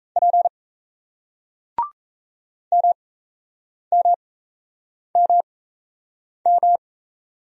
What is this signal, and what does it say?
Morse code, heard as a keyed tone, about 37 wpm, spaced as for 8 wpm, 700 Hz. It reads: P MMMM